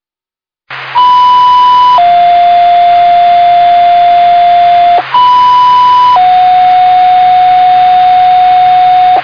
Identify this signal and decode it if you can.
Two-tone page, heard as a one-second tone, then a three-second tone, about 990 then 700 Hz, about 990 then 730 Hz. It is heard from 1.0 to 9.2 s.